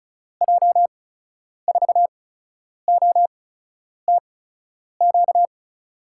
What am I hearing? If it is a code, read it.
Morse code, heard as a keyed tone, about 36 wpm, spaced as for 10 wpm, 700 Hz. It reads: J4OTQ
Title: USA Call Signs - 8x Character Spacing - 35wpm